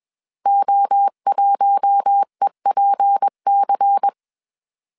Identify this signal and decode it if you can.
Morse code, heard as a keyed tone, about 21 wpm, 800 Hz. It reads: O1EPC
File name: CW / DET / 1 6 1500C S C 20wpm 2116